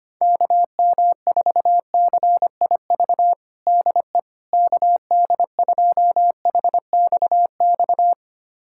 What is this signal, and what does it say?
Morse code, heard as a keyed tone, about 25 wpm, 700 Hz. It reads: KM4CIV DE KD2HXX
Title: USA Callsign DE Callsign 25wpm